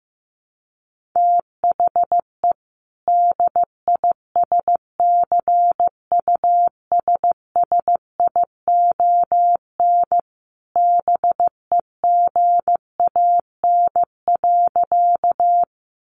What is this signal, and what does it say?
Morse code, heard as a keyed tone, about 15 wpm, 700 Hz. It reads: THE DISCUSSION BEGAN.
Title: Sentences from Top 1100 Words 15wpm